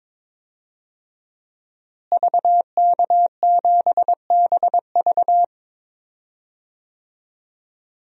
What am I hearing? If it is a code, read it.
Morse code, heard as a keyed tone, about 22 wpm, 700 Hz. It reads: VK7BV
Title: DXCC-100 Call Signs - 5 Characters - Repeated 2x - 22wpm